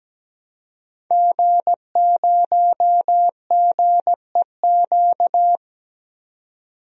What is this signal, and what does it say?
Morse code, heard as a keyed tone, about 17 wpm, 700 Hz. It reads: G0GEQ
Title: DXCC-100 Call Signs - 5 Characters - 17wpm